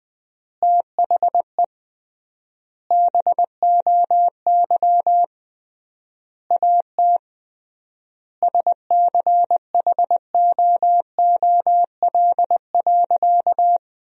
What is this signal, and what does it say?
Morse code, heard as a keyed tone, about 21 wpm, 700 Hz. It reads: THE BOY AT SCHOOL.
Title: 3x Word-Spacing - Sentences from Top 300 Words - 20wpm